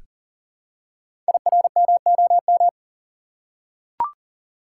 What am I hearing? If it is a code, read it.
Morse code, heard as a keyed tone, about 40 wpm, 700 Hz. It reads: IRMOM
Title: Code Groups - 5 Letters - 40wpm